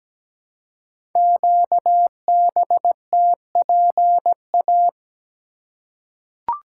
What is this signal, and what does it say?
Morse code, heard as a keyed tone, about 17 wpm, 700 Hz. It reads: QBTPA
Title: Code Groups - 5 Letters - 17wpm